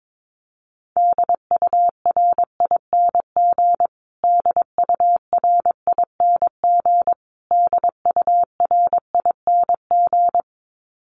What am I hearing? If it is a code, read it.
Morse code, heard as a keyed tone, about 22 wpm, 700 Hz. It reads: DURING DURING DURING
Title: Top 500 Words Repeated 22wpm